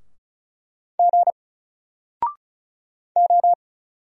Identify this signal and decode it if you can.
Morse code, heard as a keyed tone, about 34 wpm, 700 Hz. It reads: G O
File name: Single-Letter-Number-35wpm